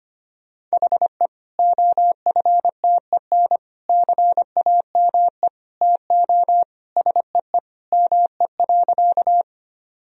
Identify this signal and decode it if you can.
Morse code, heard as a keyed tone, about 25 wpm, 700 Hz. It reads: HE OFTEN CAME TO SEE ME.
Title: Sentences from Top 1000 Words - Review Part 1 - 25wpm